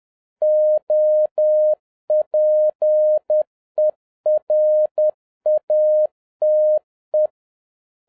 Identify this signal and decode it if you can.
Morse code, heard as a keyed tone, about 10 wpm, 610 Hz. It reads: OPERATE